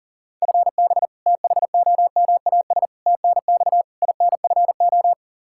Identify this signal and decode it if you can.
Morse code, heard as a keyed tone, about 40 wpm, 700 Hz. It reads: FB THOMAS TNX INFO
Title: Practice QSO - Exchange 3 - More Info - 40wpm